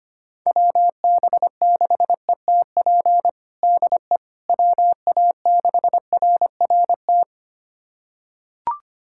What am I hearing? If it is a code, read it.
Morse code, heard as a keyed tone, about 25 wpm, 700 Hz. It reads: WB6ETP DE WA6RRT
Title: USA Callsign DE Callsign - 6 Characters - 25wpm